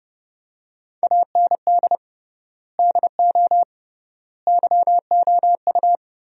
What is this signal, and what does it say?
Morse code, heard as a keyed tone, about 32 wpm, 700 Hz. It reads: AND DO YOU
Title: 3x Word-Spacing - Sentences from Top 100 Words - 30wpm